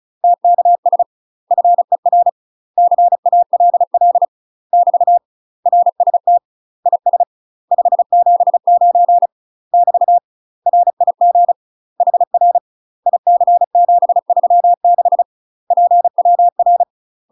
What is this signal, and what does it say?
Morse code, heard as a keyed tone, about 36 wpm, 700 Hz. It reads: TKS FER CALL <BT> RST IS 579 <BT> RIG HR IC736 PWR
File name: UK MASTER QSOs TRAINING / 35WPM / UK MASTER QSOs 0009